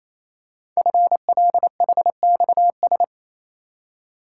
Morse code, heard as a keyed tone, about 28 wpm, 700 Hz. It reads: FLHXS
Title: Code Groups - 5 Letters - 28wpm